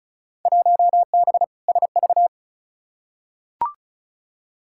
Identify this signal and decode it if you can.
Morse code, heard as a keyed tone, about 34 wpm, 700 Hz. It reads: 1B SV
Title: Field Day Exchanges - 35wpm